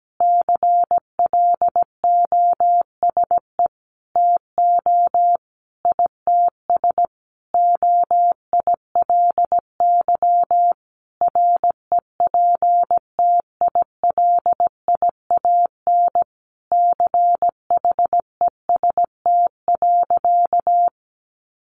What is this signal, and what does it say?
Morse code, heard as a keyed tone, about 17 wpm, 700 Hz. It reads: CLOSE TO ITS OILY REPTILIAN CHEST.
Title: A Sound Of Thunder 17wpm